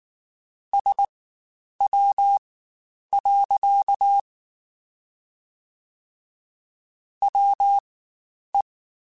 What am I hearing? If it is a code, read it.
Morse code, heard as a keyed tone, about 19 wpm, spaced as for 9 wpm, 800 Hz. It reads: SW. WE
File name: Spaced Characters / Spaced Characters-mp3s / Weather5 5 sp @18